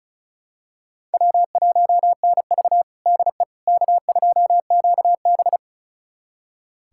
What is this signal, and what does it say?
Morse code, heard as a keyed tone, about 36 wpm, 700 Hz. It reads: W1NV DE K2QB
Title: USA Callsign DE Callsign - 4 Characters - 35wpm